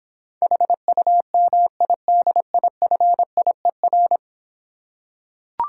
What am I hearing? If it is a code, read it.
Morse code, heard as a keyed tone, about 26 wpm, 700 Hz. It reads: HUMIDIFIER